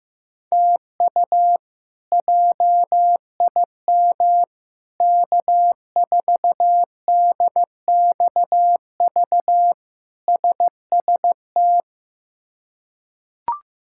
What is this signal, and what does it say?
Morse code, heard as a keyed tone, about 15 wpm, 700 Hz. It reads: TU JIM K4DXV SST
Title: K1USN SST QSO Practice - 15wpm